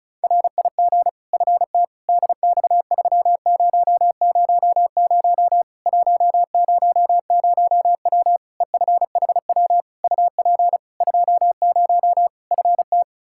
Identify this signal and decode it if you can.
Morse code, heard as a keyed tone, about 36 wpm, 700 Hz. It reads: RIG FT DX3000 100W EFHW UP 20 FT
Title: K1USN SST QSO Practice 35wpm